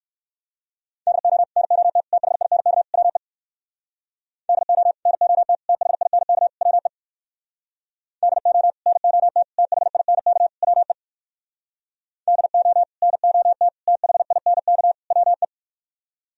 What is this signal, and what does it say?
Morse code, heard as a keyed tone, about 58 wpm, 700 Hz. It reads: DO NOT THINK WE DO NOT THINK WE DO NOT THINK WE DO NOT THINK WE